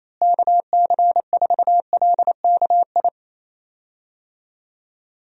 Morse code, heard as a keyed tone, about 28 wpm, 700 Hz. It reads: KC4LKI
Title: USA Call Signs - 6 Characters - Repeated 3x - 28wpm